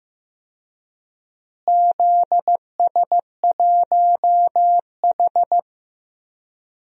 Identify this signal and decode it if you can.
Morse code, heard as a keyed tone, about 15 wpm, 700 Hz. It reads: ZS1H
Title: DXCC-100 Call Signs - 4 Characters - Repeated 3x - 15wpm